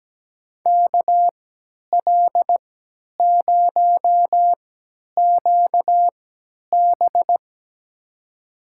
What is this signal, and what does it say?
Morse code, heard as a keyed tone, about 17 wpm, spaced as for 10 wpm, 700 Hz. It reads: KL0QB
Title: USA Call Signs - 5 Characters - 3x Character Spacing - 17wpm